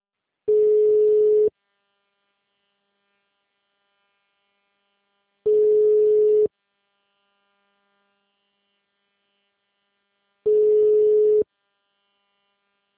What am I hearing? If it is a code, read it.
Ringback tone (Europe, 425 Hz).
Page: 154.53.36.44 Mogi Guacu - SP